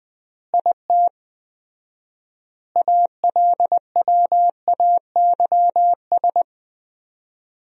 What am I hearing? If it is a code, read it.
Morse code, heard as a keyed tone, about 20 wpm, 700 Hz. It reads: IT ALWAYS
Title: 4x Word-Spacing - Sentences from Top 300 Words - 20wpm